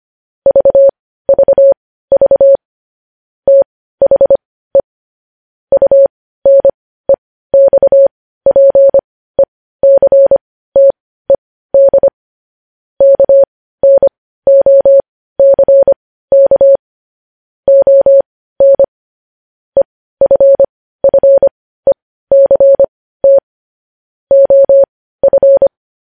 Morse code, heard as a keyed tone, about 25 wpm, spaced as for 15 wpm, 550 Hz. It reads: VVV THE UNEXPECTED KNOCK ON EFFECT OF
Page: news headlines in morse code at 15 wpm